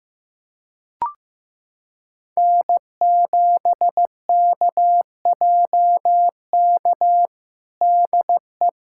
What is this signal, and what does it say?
Morse code, heard as a keyed tone, about 15 wpm, 700 Hz. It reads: N7KJK DE
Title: USA Callsign DE Callsign - 5 Characters - 15wpm